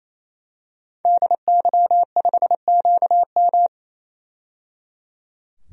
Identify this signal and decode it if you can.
Morse code, heard as a keyed tone, about 28 wpm, 700 Hz. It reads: DY5QM
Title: DXCC-100 Call Signs - 5 Characters - 28wpm